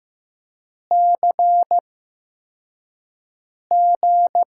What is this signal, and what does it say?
Morse code, heard as a keyed tone, about 15 wpm, spaced as for 4 wpm, 700 Hz. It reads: CG
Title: USA Call Signs - 5 Characters - 8x Character Spacing - 15wpm